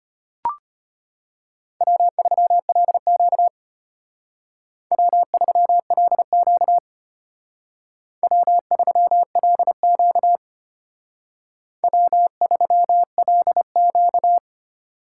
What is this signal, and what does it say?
Morse code, heard as a keyed tone, about 33 wpm, 700 Hz. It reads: W3LQ W3LQ W3LQ W3LQ